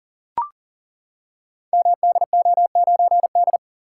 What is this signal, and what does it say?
Morse code, heard as a keyed tone, about 41 wpm, 690 Hz. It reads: MDO9D
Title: Code Groups - 5 Letter:Numbers - 40wpm